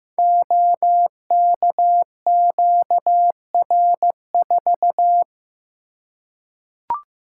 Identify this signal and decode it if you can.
Morse code, heard as a keyed tone, about 15 wpm, 700 Hz. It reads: OKQR4
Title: Code Groups - 5 Letter:Numbers - 15wpm